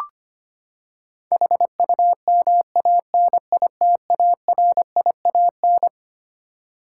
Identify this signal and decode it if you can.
Morse code, heard as a keyed tone, about 25 wpm, 700 Hz. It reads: HUMANITARIAN
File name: Twelve Letter Words - 25wpm